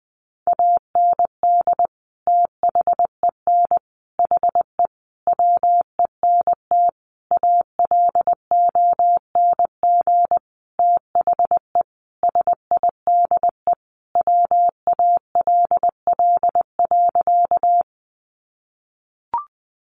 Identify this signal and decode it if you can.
Morse code, heard as a keyed tone, about 20 wpm, 700 Hz. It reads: AND THEN HE WENT ALONG THE SIDE WALL.
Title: Sentences from Top 1100 Words 20wpm